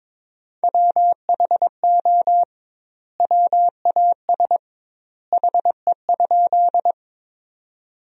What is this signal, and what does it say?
Morse code, heard as a keyed tone, about 22 wpm, 700 Hz. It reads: WHO WAS HE?
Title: 2x Word-Spacing - Sentences from Top 100 Words - 22wpm